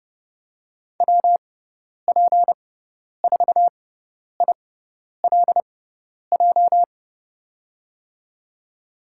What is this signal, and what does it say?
Morse code, heard as a keyed tone, about 30 wpm, spaced as for 10 wpm, 700 Hz. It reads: WP4ILJ
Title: USA Call Signs - 6 Characters - 6x Character Spacing - 30wpm